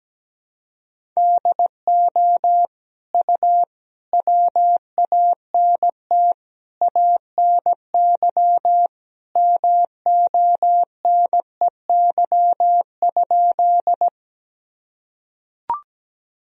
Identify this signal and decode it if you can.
Morse code, heard as a keyed tone, about 17 wpm, 700 Hz. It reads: DO U WANT ANY MONEY?
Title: Sentences from Top 400 Words and Abbreviations 17wpm